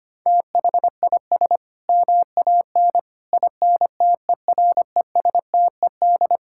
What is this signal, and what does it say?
Morse code, heard as a keyed tone, about 25 wpm, 700 Hz. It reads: THIS MAN INTERESTED
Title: Sentences from Top 1000 Words - Review Part 6 - 25wpm